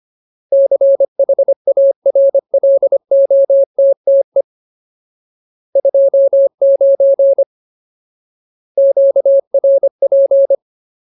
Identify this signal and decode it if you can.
Morse code, heard as a keyed tone, about 25 wpm, 550 Hz. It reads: CHARLOTTE 29 QRP
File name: Ultimate Word List / 25WPM Ultimate